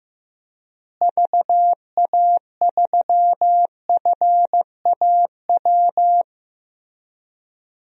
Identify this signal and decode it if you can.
Morse code, heard as a keyed tone, about 16 wpm, 700 Hz. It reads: VA3FAW
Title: DXCC-100 Call Signs - 6 Characters - Repeated 3x - 15wpm